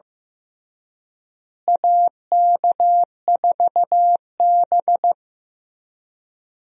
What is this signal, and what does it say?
Morse code, heard as a keyed tone, about 15 wpm, 700 Hz. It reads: AK4B
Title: USA Call Signs - 4 Characters - Repeated 3x - 15wpm